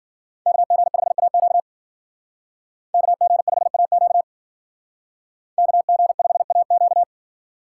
Morse code, heard as a keyed tone, about 52 wpm, 700 Hz. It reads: KG5AQ KG5AQ KG5AQ